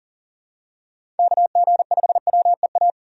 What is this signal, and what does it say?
Morse code, heard as a keyed tone, about 40 wpm, 700 Hz. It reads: KG5WEA
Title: USA Call Signs - 6 Characters - 40wpm